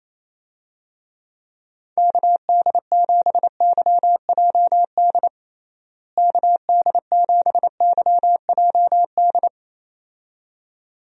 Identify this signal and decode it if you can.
Morse code, heard as a keyed tone, about 28 wpm, 700 Hz. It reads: KD7YJD KD7YJD
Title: USA Call Signs - 6 Characters - Repeated 2x - 28wpm